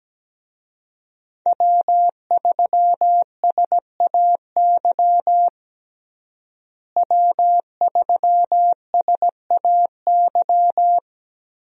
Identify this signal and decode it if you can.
Morse code, heard as a keyed tone, about 17 wpm, 700 Hz. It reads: W3SAY W3SAY
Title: USA Call Signs - 5 Characters - Repeated 2x - 17wpm